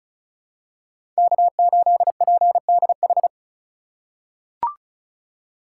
Morse code, heard as a keyed tone, about 34 wpm, 700 Hz. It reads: K8PDH
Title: USA Call Signs - 3x Character Spacing - 35wpm